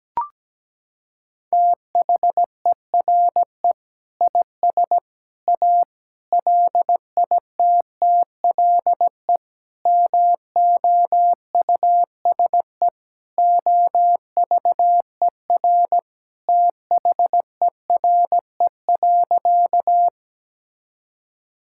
Morse code, heard as a keyed tone, about 17 wpm, 700 Hz. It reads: THERE IS A LITTLE MOUSE OVER THERE.
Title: Sentences from Top 1000 Words - Review Part 5 - 17wpm